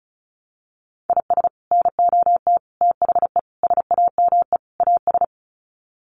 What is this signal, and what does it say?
Morse code, heard as a keyed tone, about 34 wpm, 700 Hz. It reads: IS NOT THE SAME AS
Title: Sets of 5 Words Encore 35wpm